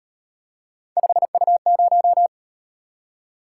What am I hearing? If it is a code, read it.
Morse code, heard as a keyed tone, about 38 wpm, 700 Hz. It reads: 5U0